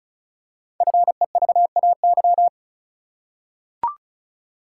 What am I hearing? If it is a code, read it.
Morse code, heard as a keyed tone, about 36 wpm, 690 Hz. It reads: FEVAY
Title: Code Groups - 5 Letter:Numbers - 35wpm